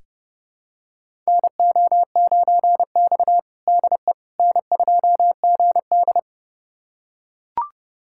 Morse code, heard as a keyed tone, about 30 wpm, 700 Hz. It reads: NO9X DE N2GD
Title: USA Callsign DE Callsign - 4 Characters - 30wpm